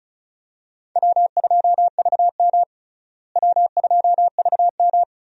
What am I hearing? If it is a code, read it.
Morse code, heard as a keyed tone, about 36 wpm, 700 Hz. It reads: W2VM W2VM
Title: USA Call Signs - 4 Characters - Repeated 3x - 35wpm